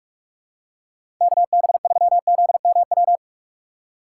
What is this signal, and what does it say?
Morse code, heard as a keyed tone, about 43 wpm, 700 Hz. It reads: KB3ZMW
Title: USA Call Signs - 6 Characters - 3x Character Spacing - 45wpm